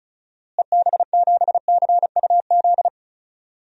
Morse code, heard as a keyed tone, about 34 wpm, 700 Hz. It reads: EB7CUZ
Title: DXCC-100 Call Signs - 6 Characters - Repeated 2x - 35wpm